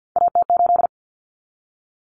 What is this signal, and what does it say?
Morse code, heard as a keyed tone, about 50 wpm, 700 Hz. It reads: AT8
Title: Common Cut Numbers - 3-Digit 50wpm